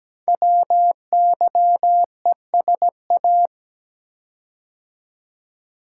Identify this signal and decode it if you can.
Morse code, heard as a keyed tone, about 17 wpm, 700 Hz. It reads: WYESA
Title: Code Groups - 5 Letters - 17wpm